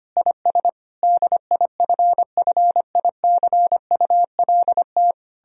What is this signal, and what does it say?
Morse code, heard as a keyed tone, about 31 wpm, 700 Hz. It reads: IS DIFFICULT
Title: Speed-Racing - Sets of 4 Words 25wpm